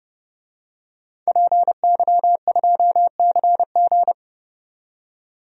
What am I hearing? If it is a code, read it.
Morse code, heard as a keyed tone, about 32 wpm, 700 Hz. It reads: PY2CG
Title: DXCC-100 Call Signs - 5 Characters - 30wpm